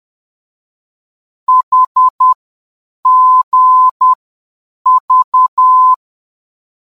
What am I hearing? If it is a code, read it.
Morse code, heard as a keyed tone, about 10 wpm, 1020 Hz. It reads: H G V